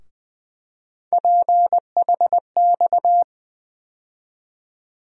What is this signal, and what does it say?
Morse code, heard as a keyed tone, about 20 wpm, 700 Hz. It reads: PHX
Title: USA Call Sign Postfixes - 3 Letters 20wpm